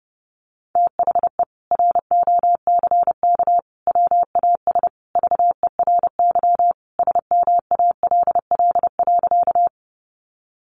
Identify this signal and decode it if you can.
Morse code, heard as a keyed tone, about 28 wpm, 700 Hz. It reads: THE ROCK WAS VERY SMALL.
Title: Sentences from Top 400 Words 30wpm